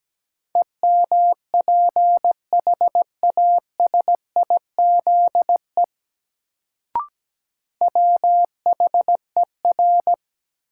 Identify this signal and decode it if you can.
Morse code, heard as a keyed tone, about 17 wpm, 700 Hz. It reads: EMPHASIZE WHER
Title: Sentences from Top 1500 Words - New Word Drill Alt - 17wpm